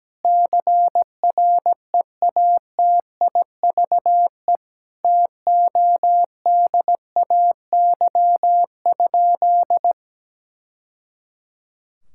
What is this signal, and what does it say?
Morse code, heard as a keyed tone, about 17 wpm, 700 Hz. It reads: CREATIVE TODAY?
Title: Sentences from Top 1000 Words - Review Part 5 - 17wpm